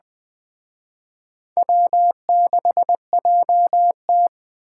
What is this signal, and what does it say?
Morse code, heard as a keyed tone, about 21 wpm, 700 Hz. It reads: W6JT
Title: Speed Racing - USA Call Signs 17wpm